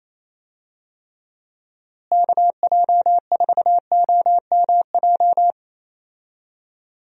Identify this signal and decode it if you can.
Morse code, heard as a keyed tone, about 28 wpm, 700 Hz. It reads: KJ4OMJ
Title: USA Call Signs - 6 Characters - Repeated 2x - 28wpm